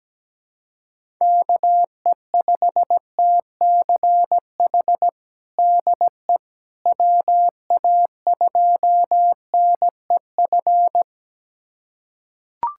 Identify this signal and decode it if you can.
Morse code, heard as a keyed tone, about 17 wpm, 700 Hz. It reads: KE5TCH DE WA2NEF
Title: USA Callsign DE Callsign 17wpm